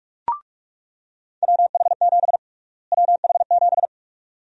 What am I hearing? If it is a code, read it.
Morse code, heard as a keyed tone, about 46 wpm, 700 Hz. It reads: WH7 WH7
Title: USA Call Sign Prefixes - Repeated 2x - 45wpm